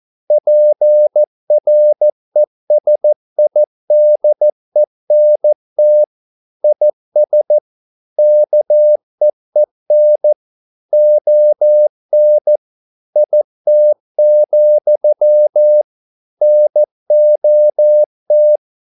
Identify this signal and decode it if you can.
Morse code, heard as a keyed tone, about 14 wpm, 600 Hz. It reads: PRESIDENT IS KEEN ON IT, NOT